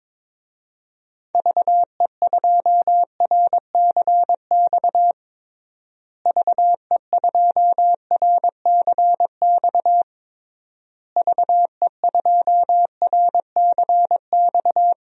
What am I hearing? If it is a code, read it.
Morse code, heard as a keyed tone, about 22 wpm, 700 Hz. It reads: VE2RCX VE2RCX VE2RCX
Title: DXCC-100 Call Signs - 6 Characters - Repeated 3x - 22wpm